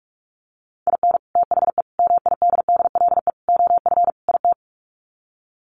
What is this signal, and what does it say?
Morse code, heard as a keyed tone, about 46 wpm, 700 Hz. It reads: IN THE MIDDLE OF IT
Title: Sets of 5 Words Encore 45wpm